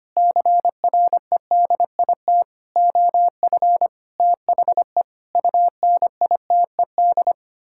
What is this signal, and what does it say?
Morse code, heard as a keyed tone, about 25 wpm, 700 Hz. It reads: CREDIT OF THE UNITED
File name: practice / 25wpm / Sentences with 9-Word Phrases - 25wpm